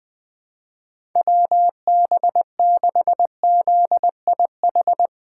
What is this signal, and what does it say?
Morse code, heard as a keyed tone, about 20 wpm, 700 Hz. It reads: WB6ZIH
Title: USA Call Signs - 6 Characters - 8x Character Spacing - 20wpm